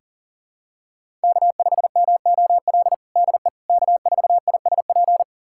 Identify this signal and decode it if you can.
Morse code, heard as a keyed tone, about 40 wpm, 700 Hz. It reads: K5MOL DE K4ISP
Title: USA Callsign DE Callsign - 5 Characters - 40wpm